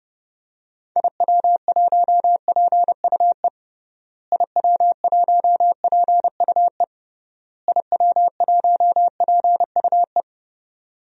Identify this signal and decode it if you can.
Morse code, heard as a keyed tone, about 28 wpm, 700 Hz. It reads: IW1PUE IW1PUE IW1PUE
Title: DXCC-100 Call Signs - 6 Characters - Repeated 3x - 30wpm